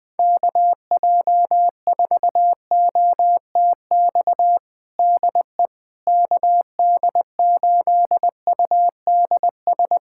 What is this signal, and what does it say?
Morse code, heard as a keyed tone, about 20 wpm, 700 Hz. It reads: KJ4OTX DE KD8UDS
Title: USA Callsign DE Callsign - 6 Characters - 20wpm